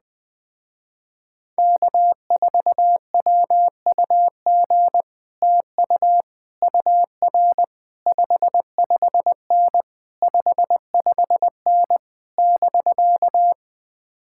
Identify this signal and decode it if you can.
Morse code, heard as a keyed tone, about 20 wpm, 700 Hz. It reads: K4WUG TU UR 55N 55N <BK>